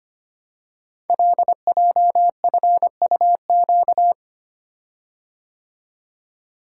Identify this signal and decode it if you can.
Morse code, heard as a keyed tone, about 25 wpm, 700 Hz. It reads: LJFUQ
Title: Code Groups - 5 Letters - 25wpm